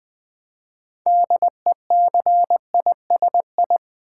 Morse code, heard as a keyed tone, about 20 wpm, 700 Hz. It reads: DECISI
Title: Build A Word - 2 Letters At A Time - 8-Letter Words - 20wpm